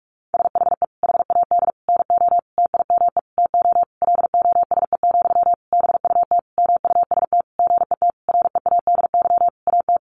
Morse code, heard as a keyed tone, about 44 wpm, 700 Hz. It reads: SHE HAD NO TIME TO LOSE, BUT MUST GET READY AT
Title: Sentences from Top 1100 Words 45wpm